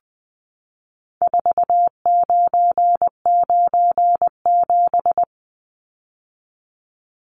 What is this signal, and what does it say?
Morse code, heard as a keyed tone, about 20 wpm, 700 Hz. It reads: 4997
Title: Numbers - 4-Digit 20wpm